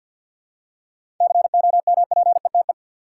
Morse code, heard as a keyed tone, about 50 wpm, 700 Hz. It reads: COMPETE
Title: Seven Letter Words - 50wpm